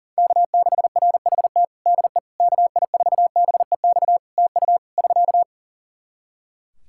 Morse code, heard as a keyed tone, about 37 wpm, 700 Hz. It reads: K6RHT DE KI4BEX TU <SK>
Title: K1USN SST QSO Practice 40wpm